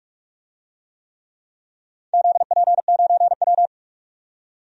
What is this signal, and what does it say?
Morse code, heard as a keyed tone, about 44 wpm, 700 Hz. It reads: ZP9W
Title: DXCC-100 Call Signs - 4 Characters - 45wpm